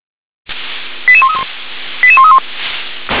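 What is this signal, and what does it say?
Five-tone selective call (ZVEI), heard as quick runs of five tones, tones of about 70 ms. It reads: 88131 88131
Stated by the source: ZVEI five-tone selective call received by SDR.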